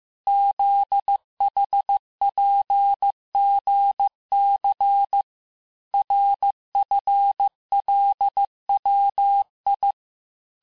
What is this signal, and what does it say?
Morse code, heard as a keyed tone, about 15 wpm, 800 Hz. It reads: ZHPGC RFLWI
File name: cours-du-lundi files / cours-1585-texte-chiffre-vitesse-14mpm